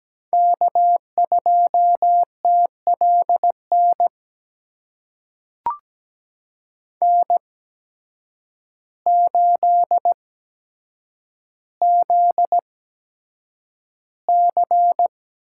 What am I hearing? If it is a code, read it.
Morse code, heard as a keyed tone, about 17 wpm, spaced as for 8 wpm, 700 Hz. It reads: K2TLN N8ZC
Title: USA Call Signs - 5 Characters - 8x Character Spacing - 17wpm